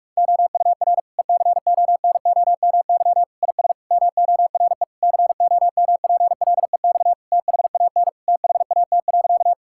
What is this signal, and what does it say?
Morse code, heard as a keyed tone, about 46 wpm, 700 Hz. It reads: OUR ECONOMY IS MORE COMPLEX THAN THAT.
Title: Sentences from Top 1000 Words - Review Part 6 - 45wpm